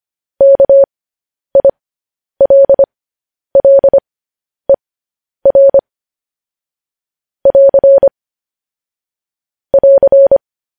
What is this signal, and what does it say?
Morse code, heard as a keyed tone, about 25 wpm, spaced as for 10 wpm, 550 Hz. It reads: KILLER <AR> <AR>
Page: News Headlines in Morse Code at 10 WPM